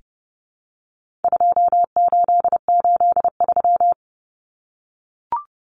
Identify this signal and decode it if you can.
Morse code, heard as a keyed tone, about 30 wpm, 700 Hz. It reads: 2883
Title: Common Cut Numbers - 4-Digit 30wpm